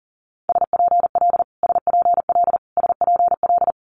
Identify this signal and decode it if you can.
Morse code, heard as a keyed tone, about 40 wpm, 700 Hz. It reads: SPL SPL SPL
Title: Multi-letter Phonemes Repeated 3x 40wpm